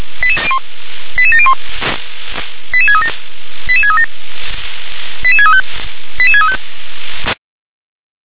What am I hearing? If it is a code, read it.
Five-tone selective call (ZVEI), heard as quick runs of five tones, tones of about 70 ms. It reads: 88701 88637 88637 88635 88635